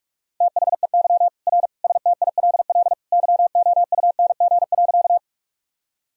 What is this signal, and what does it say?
Morse code, heard as a keyed tone, about 46 wpm, 700 Hz. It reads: THEY R STILL YOUNG.
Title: Sentences from Top 400 Words and Abbreviations 45wpm